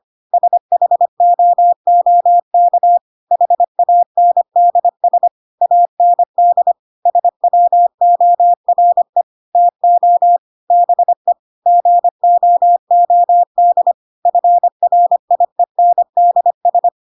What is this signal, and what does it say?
Morse code, heard as a keyed tone, about 25 wpm, 700 Hz. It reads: SHOOK HANDS AND SWORE TO BE GOOD FRIENDS
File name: Pinocchio / Pinocchio ch2 25 25 ik1hge